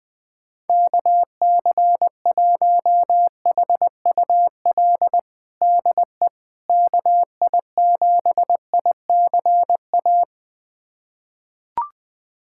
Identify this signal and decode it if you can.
Morse code, heard as a keyed tone, about 20 wpm, 700 Hz. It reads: KC1HUL DE KI7ICA
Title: USA Callsign DE Callsign - 6 Characters - 20wpm